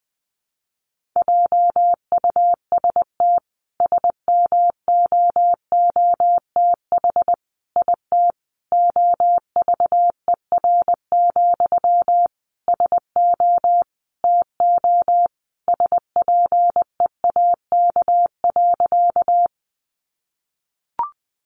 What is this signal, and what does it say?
Morse code, heard as a keyed tone, about 20 wpm, 700 Hz. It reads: JUST SMOOTH IT OVER, SO TO SPEAK.
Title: Sentences from Top 1400 Words 20wpm